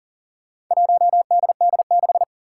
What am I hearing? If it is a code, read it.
Morse code, heard as a keyed tone, about 40 wpm, 700 Hz. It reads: 1DD6
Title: Four Challenging Letter-Number Combinations 40wpm